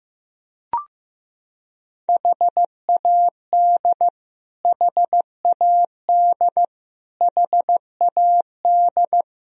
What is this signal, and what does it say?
Morse code, heard as a keyed tone, about 15 wpm, 700 Hz. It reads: HAD HAD HAD
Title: Top 100 Words Repeated 15wpm